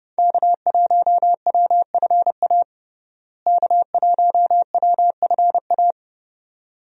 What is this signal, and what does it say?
Morse code, heard as a keyed tone, about 30 wpm, 700 Hz. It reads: K1WFA K1WFA
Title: USA Call Signs - 5 Characters - Repeated 2x - 30wpm